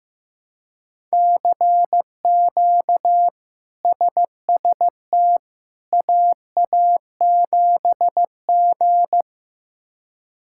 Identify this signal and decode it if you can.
Morse code, heard as a keyed tone, about 16 wpm, 700 Hz. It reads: CQ SST AA7G
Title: K1USN SST QSO Practice - 15wpm